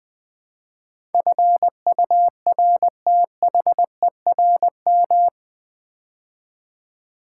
Morse code, heard as a keyed tone, about 21 wpm, 700 Hz. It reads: FURTHERM
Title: Build A Word - 4 Letters At A Time - 11-Letter Words - 20wpm